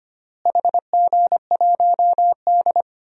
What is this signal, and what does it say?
Morse code, heard as a keyed tone, about 25 wpm, 700 Hz. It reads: HG1D
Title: DXCC-100 Call Signs - 4 Characters - Repeated 2x - 25wpm